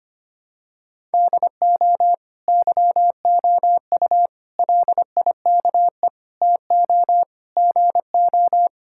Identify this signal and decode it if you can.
Morse code, heard as a keyed tone, about 25 wpm, 700 Hz. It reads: DO YOU LIKE TO GO
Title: Sentences from Top 1000 Words - Review Part 5 - 25wpm